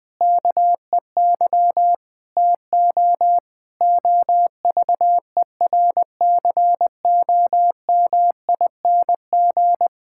Morse code, heard as a keyed tone, about 20 wpm, 700 Hz. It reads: KEY TO OVERCOMING
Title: Difficult Sentences with Hints - 20wpm